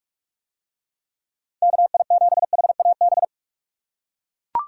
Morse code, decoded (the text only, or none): KI7HAB